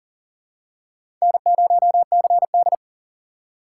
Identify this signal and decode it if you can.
Morse code, heard as a keyed tone, about 40 wpm, 700 Hz. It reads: N0CD
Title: USA Call Signs - 4 Characters - 4x Character Spacing - 40wpm